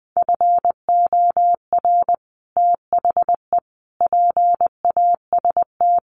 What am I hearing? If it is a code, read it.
Morse code, heard as a keyed tone, about 19 wpm, 700 Hz. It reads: FOR THE PAST
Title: Sets of 5 Words Encore2 20wpm